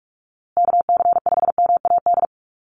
Morse code, heard as a keyed tone, about 44 wpm, 700 Hz. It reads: KC5MAD
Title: USA Call Signs Repeated 3x 45wpm